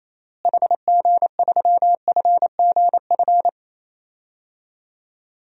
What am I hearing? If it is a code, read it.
Morse code, heard as a keyed tone, about 27 wpm, 700 Hz. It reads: HG3FGF
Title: DXCC-100 Call Signs - 6 Characters - Repeated 3x - 28wpm